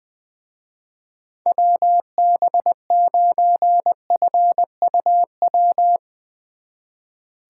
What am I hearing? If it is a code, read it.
Morse code, heard as a keyed tone, about 22 wpm, 700 Hz. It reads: WB9FUW
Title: Speed Racing - USA 2x3 Call Signs 20wpm